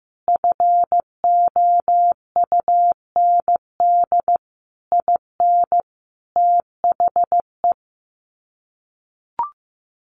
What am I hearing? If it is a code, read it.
Morse code, heard as a keyed tone, about 15 wpm, 700 Hz. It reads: FOUND IN THE
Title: Sets of 5 Words Encore 15wpm